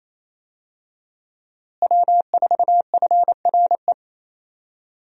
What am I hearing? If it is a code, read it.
Morse code, heard as a keyed tone, about 28 wpm, 700 Hz. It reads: W4FRE
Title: USA Call Signs - 5 Characters - Repeated 3x - 28wpm